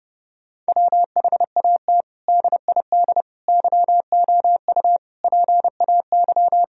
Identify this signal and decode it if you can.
Morse code, heard as a keyed tone, about 30 wpm, 700 Hz. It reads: WHAT DID YOU PAY
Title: Sentences from Top 1000 Words - Review Part 3 - 30wpm